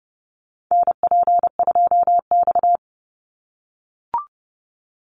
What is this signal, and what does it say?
Morse code, heard as a keyed tone, about 30 wpm, 700 Hz. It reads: NP2X
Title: Call Signs - 30wpm